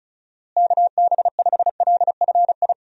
Morse code, heard as a keyed tone, about 36 wpm, 700 Hz. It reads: KB5LFI
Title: USA Call Signs - 6 Characters - Repeated 3x - 35wpm